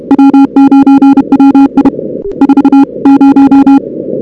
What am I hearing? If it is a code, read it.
Morse code, heard as a keyed tone, about 31 wpm, 280 Hz. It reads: W9WI 40
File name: CWSprint / Feb2006 / 2006-feb-cw-sprint-k5tr op-n5rz -02:00